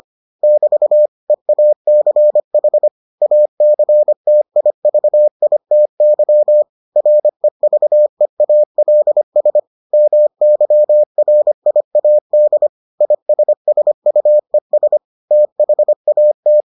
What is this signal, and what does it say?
Morse code, heard as a keyed tone, about 25 wpm, 600 Hz. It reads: <BT> EACH ACTIVITY REVEALS MYRIAD ISSUES THAT